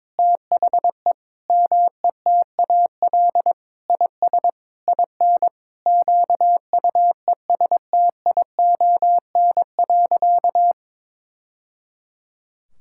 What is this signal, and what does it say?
Morse code, heard as a keyed tone, about 22 wpm, 700 Hz. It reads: THE METAL IS IN QUESTION.
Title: Sentences from Top 1000 Words - Review Part 6 - 22wpm